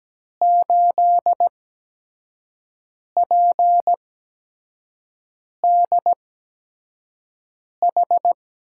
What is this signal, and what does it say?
Morse code, heard as a keyed tone, about 17 wpm, spaced as for 5 wpm, 700 Hz. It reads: 8PDH
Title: USA Call Signs - 5 Characters - 8x Character Spacing - 17wpm